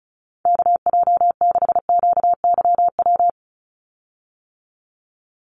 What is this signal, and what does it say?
Morse code, heard as a keyed tone, about 34 wpm, 700 Hz. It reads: KJ6QYW
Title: USA Call Signs 35wpm